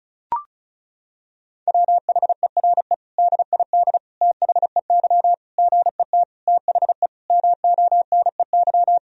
Morse code, heard as a keyed tone, about 34 wpm, 700 Hz. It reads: WHERE DID THEY GET THE MONEY
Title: Sentences from Top 400 Words and Abbreviations 35wpm